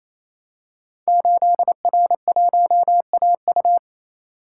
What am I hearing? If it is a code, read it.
Morse code, heard as a keyed tone, about 29 wpm, 700 Hz. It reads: 8R1AU
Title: Code Groups - 5 Letter:Numbers - 28wpm